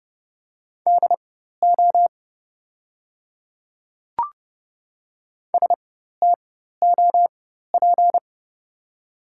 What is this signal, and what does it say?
Morse code, heard as a keyed tone, about 30 wpm, spaced as for 14 wpm, 700 Hz. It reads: DO STOP
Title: 4x Character Spacing - Top 300 Words - 30wpm